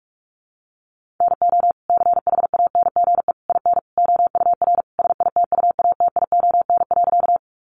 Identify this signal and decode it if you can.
Morse code, heard as a keyed tone, about 44 wpm, 700 Hz. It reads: NO CHANGE IN OUR SITUATION.
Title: Sentences from Top 1200 Words 45wpm